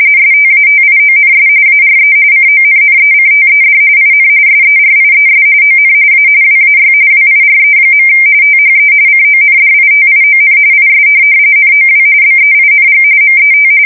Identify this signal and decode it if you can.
RTTY, about 45.45 baud. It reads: STATIONS ARE CONNECTED IN SERIES ON A WAY WIRE. THE CONDITION IS NOW BEING MET
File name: fsk170 / printing-telegraph